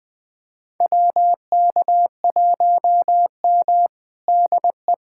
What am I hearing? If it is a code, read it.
Morse code, heard as a keyed tone, about 21 wpm, 700 Hz. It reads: WK1M DE
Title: USA Callsign DE Callsign - 4 Characters - 20wpm